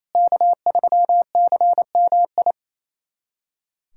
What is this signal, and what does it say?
Morse code, heard as a keyed tone, about 28 wpm, 700 Hz. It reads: K3CMI
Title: USA Callsign DE Callsign 28wpm